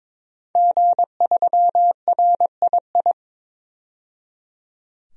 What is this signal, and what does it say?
Morse code, heard as a keyed tone, about 23 wpm, 700 Hz. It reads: G3RII
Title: DXCC-100 Call Signs - 5 Characters - 22wpm